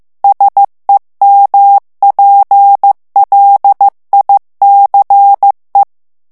Morse code, heard as a keyed tone, about 15 wpm, 800 Hz. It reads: SEMPLICE
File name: morse mp3 / ITALIEN texte clair 2 vitesse 14 mots minute